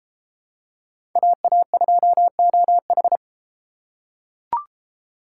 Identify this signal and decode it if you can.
Morse code, heard as a keyed tone, about 30 wpm, 700 Hz. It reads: AA2OH